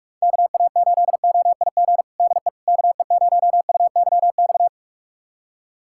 Morse code, heard as a keyed tone, about 46 wpm, 700 Hz. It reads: KA8OIG DE KE0UYX
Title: USA Callsign DE Callsign - 6 Characters - 45wpm